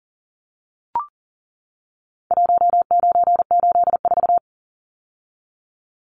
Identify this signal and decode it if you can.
Morse code, heard as a keyed tone, about 40 wpm, 700 Hz. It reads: 1984
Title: Numbers - Contest Style - 4-Digit 40wpm